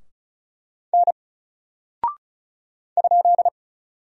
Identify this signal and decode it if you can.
Morse code, heard as a keyed tone, about 36 wpm, 700 Hz. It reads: N ?
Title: Single Character (Letter:Number:Punctuation:Prosign) Alt 35wpm